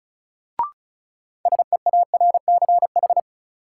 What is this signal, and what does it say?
Morse code, heard as a keyed tone, about 36 wpm, 700 Hz. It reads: SEARCH
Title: Sentences from Top 800 Words - New Word Drill - 35wpm